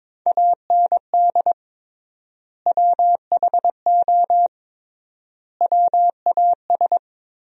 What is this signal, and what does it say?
Morse code, heard as a keyed tone, about 22 wpm, 700 Hz. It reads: AND WHO WAS
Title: 3x Word-Spacing - Sentences from Top 100 Words - 22wpm